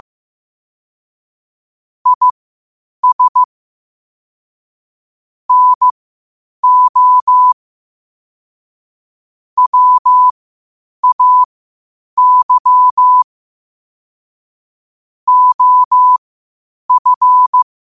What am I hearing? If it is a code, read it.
Morse code, heard as a keyed tone, about 15 wpm, spaced as for 8 wpm, 1000 Hz. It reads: IS NO WAY OF
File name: 8wpm